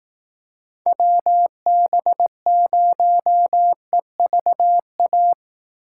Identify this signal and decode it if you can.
Morse code, heard as a keyed tone, about 18 wpm, 700 Hz. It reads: WB0EVA
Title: Speed Racing - USA 2x3 Call Signs 15wpm